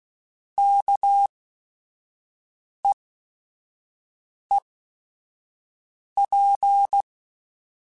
Morse code, heard as a keyed tone, about 16 wpm, spaced as for 5 wpm, 800 Hz. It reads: KEEP
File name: Chuckles / Chuckles-mp3s / SoftWords @5